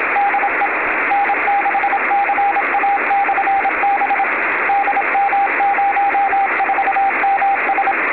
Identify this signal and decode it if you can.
Morse code, heard as a keyed tone, about 27 wpm, 800 Hz, with radio noise. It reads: DE N6CA/B DM04MS
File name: beacons / Frzr / fzrbcn